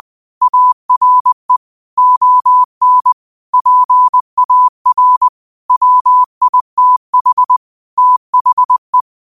Morse code, heard as a keyed tone, about 20 wpm, 1000 Hz. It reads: ARE ON PAR WITH THE
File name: qotdfiles / 20wpm